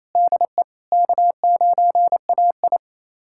Morse code, heard as a keyed tone, about 28 wpm, 700 Hz. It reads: DE K9AI
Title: USA Callsign DE Callsign - 4 Characters - 28wpm